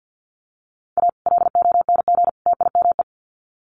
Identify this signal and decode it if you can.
Morse code, heard as a keyed tone, about 50 wpm, 700 Hz. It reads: A LONG TIME
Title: Sets of 3 Words 50wpm